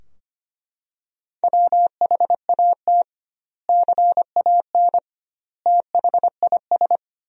Morse code, heard as a keyed tone, about 25 wpm, 700 Hz. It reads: WHAT CAN THIS
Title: 2x Word-Spacing - Sentences from Top 100 Words - 25wpm